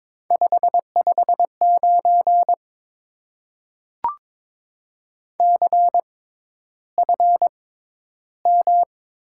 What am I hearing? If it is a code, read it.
Morse code, heard as a keyed tone, about 22 wpm, spaced as for 11 wpm, 700 Hz. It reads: 559 CFM
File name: Top 90 QSO Elements - 6x Character Spacing - 22wpm